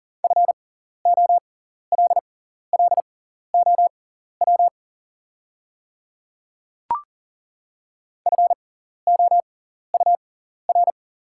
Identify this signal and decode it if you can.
Morse code, heard as a keyed tone, about 40 wpm, spaced as for 14 wpm, 690 Hz. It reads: FOLLOW FOUR
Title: 6x Character Spacing - Top 500 Words - 40wpm